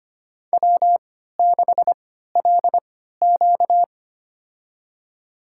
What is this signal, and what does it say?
Morse code, heard as a keyed tone, about 25 wpm, spaced as for 14 wpm, 700 Hz. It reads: W6LQ
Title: USA Call Signs - 4 Characters - 3x Character Spacing - 25wpm